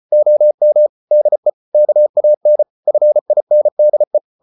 Morse code, heard as a keyed tone, about 34 wpm, 600 Hz. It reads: OM DE KAN FINDE
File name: Ferie QRQ 34wpm